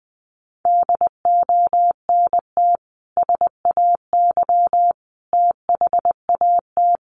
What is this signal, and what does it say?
Morse code, heard as a keyed tone, about 20 wpm, 700 Hz. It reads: DONT SAY THAT
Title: Sentences from Top 300 Words 20wpm